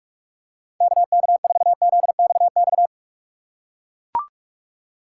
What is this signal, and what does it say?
Morse code, heard as a keyed tone, about 43 wpm, 700 Hz. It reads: KK4ZXX
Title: USA Call Signs - 4x Character Spacing - 45wpm